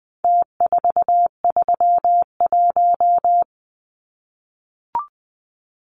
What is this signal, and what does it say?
Morse code, heard as a keyed tone, about 20 wpm, 700 Hz. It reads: T431
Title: Numbers - Contest Style - 4-Digit 20wpm